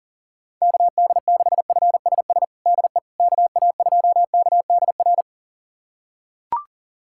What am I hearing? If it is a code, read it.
Morse code, heard as a keyed tone, about 40 wpm, 700 Hz. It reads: KD6FSS DE KA2KDR
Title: USA Callsign DE Callsign 40wpm